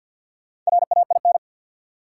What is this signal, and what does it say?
Morse code, heard as a keyed tone, about 53 wpm, 700 Hz. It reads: RAIN
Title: Sentences from Top 600 Words - New Word Drill - 50wpm